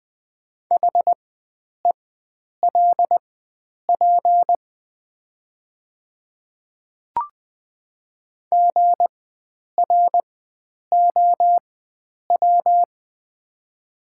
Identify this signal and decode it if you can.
Morse code, heard as a keyed tone, about 20 wpm, spaced as for 9 wpm, 700 Hz. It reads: HELP GROW
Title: 4x Character Spacing - Top 300 Words - 20wpm